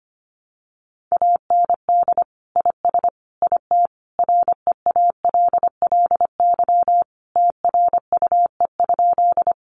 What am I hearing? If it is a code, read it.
Morse code, heard as a keyed tone, about 25 wpm, 700 Hz. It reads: AND IS IT REALLY TRUE?
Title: Sentences from Top 600 Words 25wpm